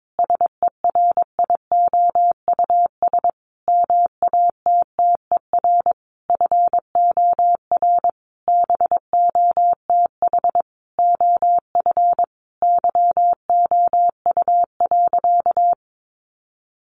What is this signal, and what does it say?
Morse code, heard as a keyed tone, about 22 wpm, 700 Hz. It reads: SERIOUS MATTER FOR BOTH OF YOU.
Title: Sentences from Top 1100 Words 22wpm